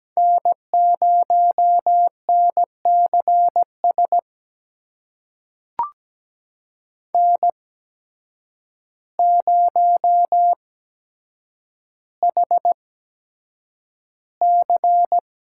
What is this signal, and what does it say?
Morse code, heard as a keyed tone, about 17 wpm, spaced as for 8 wpm, 700 Hz. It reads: N0NCS N0HC
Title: USA Call Signs - 5 Characters - 8x Character Spacing - 17wpm